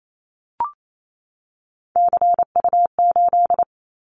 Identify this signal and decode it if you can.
Morse code, heard as a keyed tone, about 28 wpm, 700 Hz. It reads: CU8
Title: Callsign Prefixes for Top 50 DXCC - 28wpm